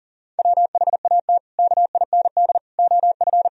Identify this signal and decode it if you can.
Morse code, heard as a keyed tone, about 40 wpm, 700 Hz. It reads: WHAT KIND OF
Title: Sentences from Top 1000 Words - Review Part 4 - 40wpm